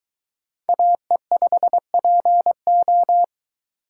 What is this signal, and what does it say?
Morse code, heard as a keyed tone, about 23 wpm, 700 Hz. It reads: AE5PO